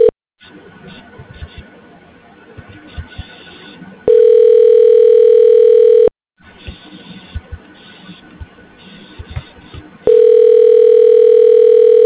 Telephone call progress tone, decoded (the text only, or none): ringback tone